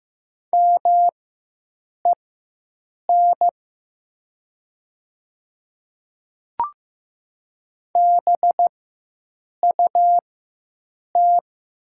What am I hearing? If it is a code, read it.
Morse code, heard as a keyed tone, about 15 wpm, spaced as for 7 wpm, 700 Hz. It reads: MEN BUT
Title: 4x Character Spacing - Top 200 Words - 15wpm